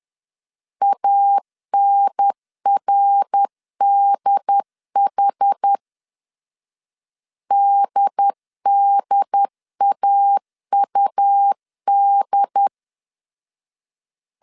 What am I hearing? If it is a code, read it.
Morse code, heard as a keyed tone, about 10 wpm, 800 Hz. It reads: ANRDH DDAUD
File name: CW / DET / 1 3 1500C S C 10wpm 3335